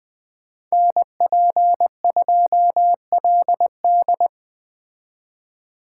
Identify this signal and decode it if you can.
Morse code, heard as a keyed tone, about 20 wpm, 700 Hz. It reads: NP2LD
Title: Speed Racing - USA Call Signs 17wpm